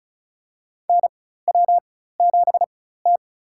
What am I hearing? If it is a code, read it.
Morse code, heard as a keyed tone, about 36 wpm, spaced as for 16 wpm, 700 Hz. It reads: NW7T
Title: USA Call Signs - 4 Characters - 4x Character Spacing - 35wpm